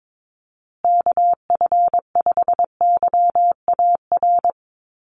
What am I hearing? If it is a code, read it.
Morse code, heard as a keyed tone, about 22 wpm, 700 Hz. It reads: KF5YAR
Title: USA Call Signs Repeated 3x 22wpm